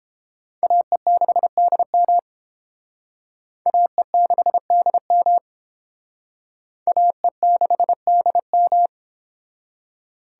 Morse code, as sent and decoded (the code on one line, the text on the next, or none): .- . -.... -.. -- / .- . -.... -.. -- / .- . -.... -.. --
AE6DM AE6DM AE6DM